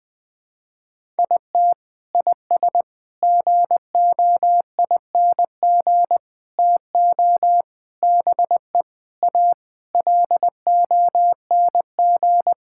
Morse code, heard as a keyed tone, about 21 wpm, 700 Hz. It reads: IT IS GOING TO BE A LONG
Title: Sentences with 4-Word Phrases - 20wpm